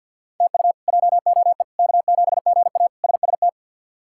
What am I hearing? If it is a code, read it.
Morse code, heard as a keyed tone, about 46 wpm, 700 Hz. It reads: TU JOE K7GA SST
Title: K1USN SST QSO Practice - 50wpm